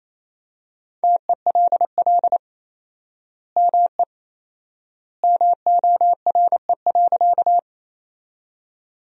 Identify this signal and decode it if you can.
Morse code, heard as a keyed tone, about 28 wpm, 700 Hz. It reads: TELL ME MORE.
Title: 4x Word-Spacing - Sentences from Top 300 Words - 28wpm